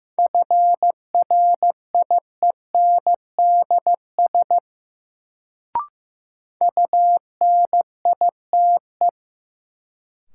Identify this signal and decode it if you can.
Morse code, heard as a keyed tone, about 15 wpm, 700 Hz. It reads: FRIENDS UNITE
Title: Sentences from Top 1000 Words - Review Part 5 - New Word Drill - 15wpm